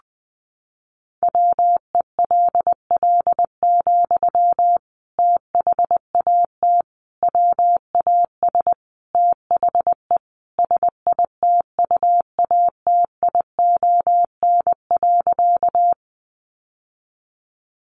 Morse code, heard as a keyed tone, about 19 wpm, 700 Hz. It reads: WELL, THAT WAS THE SITUATION.
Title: Sentences from Top 1200 Words 20wpm